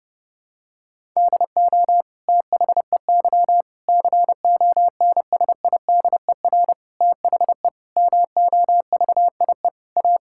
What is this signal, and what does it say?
Morse code, heard as a keyed tone, about 30 wpm, 700 Hz. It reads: DO THEY CONSIDER THE MOVIE A
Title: Sentences from Top 1000 Words - Review Part 6 - 30wpm